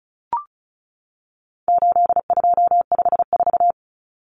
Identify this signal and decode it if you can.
Morse code, heard as a keyed tone, about 34 wpm, 700 Hz. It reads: 8254
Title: Common Cut Numbers - 4-Digit 35wpm